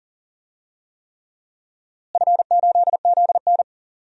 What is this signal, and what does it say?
Morse code, heard as a keyed tone, about 40 wpm, 700 Hz. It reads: F8ZN
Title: DXCC-100 Call Signs - 4 Characters - Repeated 3x - 40wpm